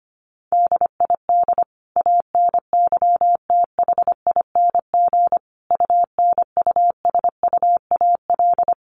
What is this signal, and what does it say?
Morse code, heard as a keyed tone, about 25 wpm, 700 Hz. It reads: DID ANYTHING UNUSUAL
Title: Sentences from Top 1400 Words 25wpm